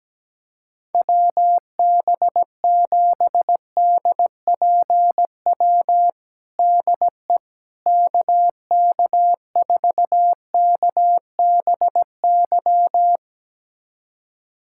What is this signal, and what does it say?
Morse code, heard as a keyed tone, about 17 wpm, 700 Hz. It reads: WB7DPW DE KK4KBY
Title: USA Callsign DE Callsign - 6 Characters - 17wpm